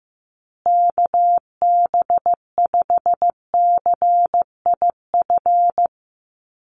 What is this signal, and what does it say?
Morse code, heard as a keyed tone, about 15 wpm, 700 Hz. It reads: KB5CIF
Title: USA Call Signs Repeated 3x 15wpm